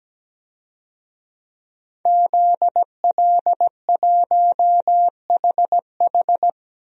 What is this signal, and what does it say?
Morse code, heard as a keyed tone, about 17 wpm, 700 Hz. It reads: ZL1HH
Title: DXCC-100 Call Signs - 5 Characters - 17wpm